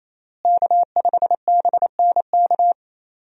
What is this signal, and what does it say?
Morse code, heard as a keyed tone, about 28 wpm, 700 Hz. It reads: K5BNK
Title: USA Call Signs - 5 Characters - Repeated 3x - 28wpm